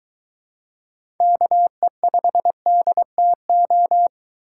Morse code, heard as a keyed tone, about 21 wpm, 700 Hz. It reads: KE5DTO